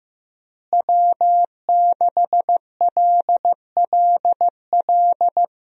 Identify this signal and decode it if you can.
Morse code, heard as a keyed tone, about 15 wpm, 700 Hz. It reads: W6LLL
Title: USA Call Signs - 5 Characters - Repeated 3x - 15wpm